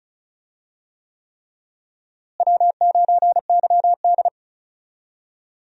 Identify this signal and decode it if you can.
Morse code, heard as a keyed tone, about 34 wpm, 700 Hz. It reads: W9YD
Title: USA Call Signs - 4 Characters - Repeated 3x - 35wpm